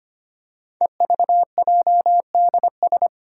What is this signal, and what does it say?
Morse code, heard as a keyed tone, about 25 wpm, 700 Hz. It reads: EVJDS
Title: Code Groups - 5 Letters - 25wpm